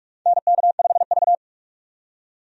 Morse code, heard as a keyed tone, about 46 wpm, 700 Hz. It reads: NK5V
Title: USA Call Signs - 4 Characters - 45wpm